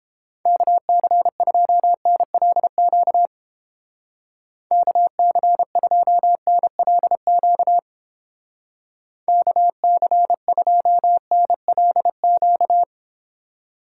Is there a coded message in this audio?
Yes: Morse code from 0.5 to 12.8 s, heard as a keyed tone, about 29 wpm, 700 Hz. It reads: KC2NLQ KC2NLQ KC2NLQ